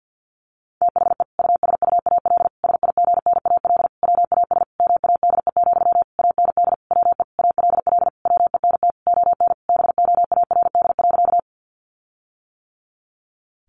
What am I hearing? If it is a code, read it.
Morse code, heard as a keyed tone, about 50 wpm, 700 Hz. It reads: THE USUAL SIGNAL WAS MADE, AND WE ALL WENT ON BOARD.
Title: Sentences from Top 1400 Words 50wpm